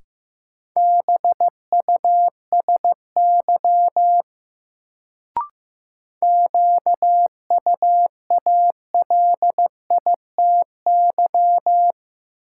Morse code, heard as a keyed tone, about 15 wpm, 700 Hz. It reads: BUSY QUALITY
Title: Sentences from Top 1500 Words - New Word Drill Alt - 15wpm